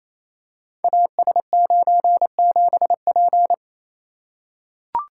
Morse code, heard as a keyed tone, about 28 wpm, 700 Hz. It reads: AS97P
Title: Code Groups - 5 Letter:Numbers - 28wpm